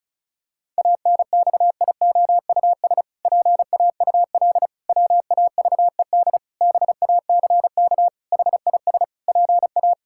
Morse code, heard as a keyed tone, about 36 wpm, 700 Hz. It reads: ANXIOUS PAUL WAVED BACK HIS PA
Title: Pangram Challenge v2 - 35wpm